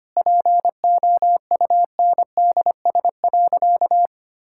Morse code, heard as a keyed tone, about 25 wpm, 700 Hz. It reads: POUNDS.
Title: Sentences from Top 1000 Words - Review Part 6 - 25wpm